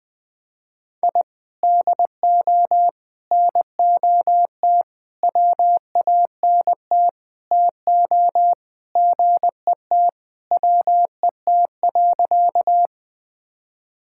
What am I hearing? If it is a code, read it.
Morse code, heard as a keyed tone, about 20 wpm, 700 Hz. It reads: I DO NOT WANT TO GET WET.
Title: Sentences with 5-Word Phrases - 20wpm